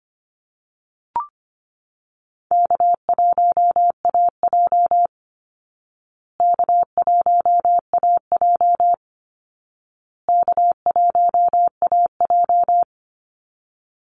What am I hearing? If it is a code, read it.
Morse code, heard as a keyed tone, about 25 wpm, 700 Hz. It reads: K1AJ K1AJ K1AJ
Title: Easy Call Signs Repeated 3x - 25wpm